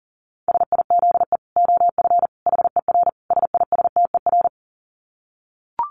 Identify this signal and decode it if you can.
Morse code, heard as a keyed tone, about 40 wpm, 700 Hz. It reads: SIZE OF HER SISTER
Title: Dit Challenge 40wpm